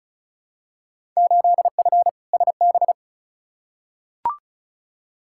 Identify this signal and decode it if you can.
Morse code, heard as a keyed tone, about 36 wpm, 690 Hz. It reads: 8F SB
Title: Field Day Exchanges Encore - 35wpm